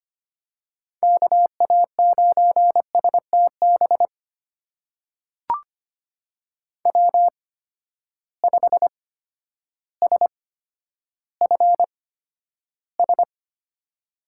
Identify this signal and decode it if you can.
Morse code, heard as a keyed tone, about 25 wpm, spaced as for 11 wpm, 700 Hz. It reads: KA9STB W5SFS
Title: USA Call Signs - 8x Character Spacing - 25wpm